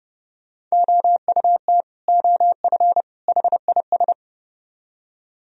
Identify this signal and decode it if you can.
Morse code, heard as a keyed tone, about 38 wpm, 700 Hz. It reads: OUT OF HIS